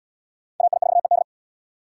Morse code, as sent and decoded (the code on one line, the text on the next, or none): .- .. ....- . -..
AI4ED